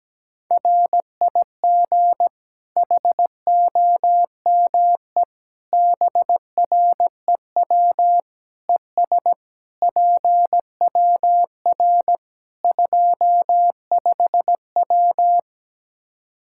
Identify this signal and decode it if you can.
Morse code, heard as a keyed tone, about 17 wpm, 700 Hz. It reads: RIG HOME BREW ES PWR 25W
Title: Practice QSO - Exchange 2 - Info - 17wpm